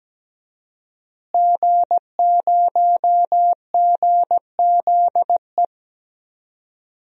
Morse code, heard as a keyed tone, about 17 wpm, 700 Hz. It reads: G0GZE
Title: DXCC-100 Call Signs - 5 Characters - Repeated 3x - 17wpm